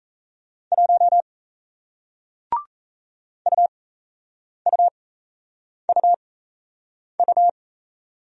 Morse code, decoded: 1 UUUU

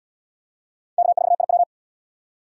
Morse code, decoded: K4IW

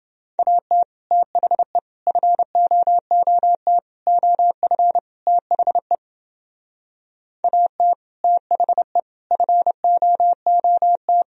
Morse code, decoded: AT THE FOOT OF THE AT THE FOOT

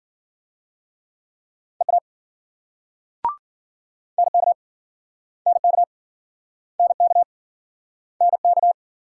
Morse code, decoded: EA NK NK NK NK

